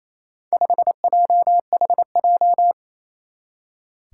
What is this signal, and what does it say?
Morse code, heard as a keyed tone, about 28 wpm, 700 Hz. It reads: 5JHJ
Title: Four Challenging Letter-Number Combinations 28wpm